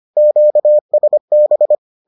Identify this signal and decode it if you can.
Morse code, heard as a keyed tone, about 25 wpm, 600 Hz. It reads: QSB